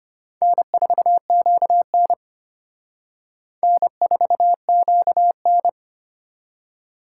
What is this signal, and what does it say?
Morse code, heard as a keyed tone, about 34 wpm, 700 Hz. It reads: N4QN N4QN